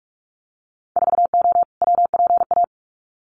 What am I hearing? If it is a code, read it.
Morse code, heard as a keyed tone, about 44 wpm, 700 Hz. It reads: 4O WPA
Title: Winter Field Day Exchanges 45wpm